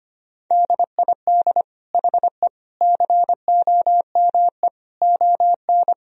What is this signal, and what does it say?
Morse code, heard as a keyed tone, about 25 wpm, 700 Hz. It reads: DID HE COME ON
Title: Sentences from Top 1000 Words - Review Part 4 - 25wpm